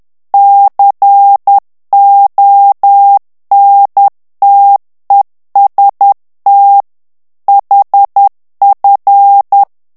Morse code, heard as a keyed tone, about 10 wpm, 800 Hz. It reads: CONTEST HF
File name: ITALIEN texte clair 2 vitesse 10 mots minute